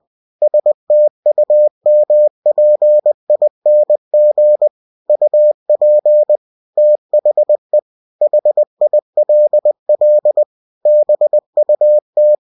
Morse code, heard as a keyed tone, about 20 wpm, 600 Hz. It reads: STUMPING UP THE HILL BUT
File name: cw-books / winnie-the-pooh / winnie-20wpm-page-0066